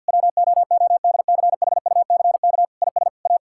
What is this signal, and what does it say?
Morse code, heard as a keyed tone, about 51 wpm, 700 Hz. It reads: WOODCHUCK IS A
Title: Pangram Challenge v2 - 50wpm